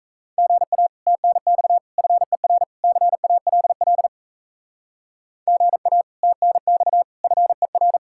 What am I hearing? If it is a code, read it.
Morse code, heard as a keyed tone, about 34 wpm, 700 Hz. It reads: GA TNX FER CALL GA TNX FER
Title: Speed-Racing Common QSO Phrases - 28wpm